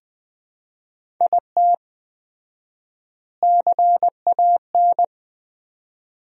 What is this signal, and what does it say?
Morse code, heard as a keyed tone, about 20 wpm, 700 Hz. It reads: IT CAN
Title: 4x Word-Spacing - Sentences from Top 400 Words - 20wpm